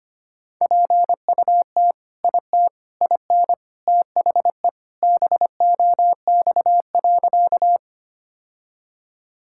Morse code, heard as a keyed tone, about 25 wpm, 700 Hz. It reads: PUT IT IN THE BOX.
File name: practice / 25wpm / Sentences with 3-Word Phrases - 25wpm